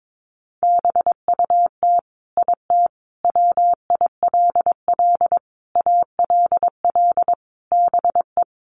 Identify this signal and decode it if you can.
Morse code, heard as a keyed tone, about 22 wpm, 700 Hz. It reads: BUT IT WILL ALL BE
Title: Sentences from Top 1400 Words 22wpm